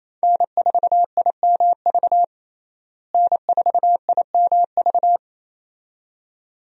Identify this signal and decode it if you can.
Morse code, heard as a keyed tone, about 28 wpm, 700 Hz. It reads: N4IMV N4IMV
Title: USA Call Signs - 5 Characters - Repeated 2x - 28wpm